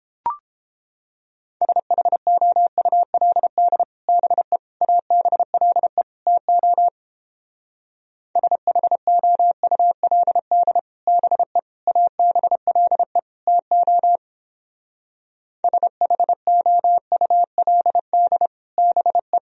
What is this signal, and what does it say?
Morse code, heard as a keyed tone, about 29 wpm, 700 Hz. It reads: SHOULD BE ABLE TO SHOULD BE ABLE TO SHOULD BE